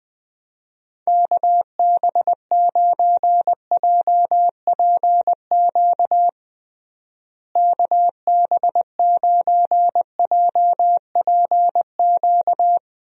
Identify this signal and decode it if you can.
Morse code, heard as a keyed tone, about 20 wpm, 700 Hz. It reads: KB9JPQ KB9JPQ
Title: USA Call Signs - 6 Characters - Repeated 3x - 20wpm